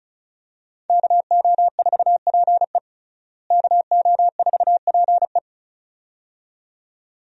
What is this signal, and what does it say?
Morse code, heard as a keyed tone, about 34 wpm, 700 Hz. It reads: KO4PE KO4PE
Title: USA Call Signs - 5 Characters - Repeated 2x - 35wpm